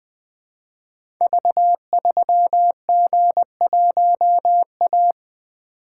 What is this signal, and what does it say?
Morse code, heard as a keyed tone, about 20 wpm, 700 Hz. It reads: V3G1A
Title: Code Groups - 5 Letter:Numbers - 20wpm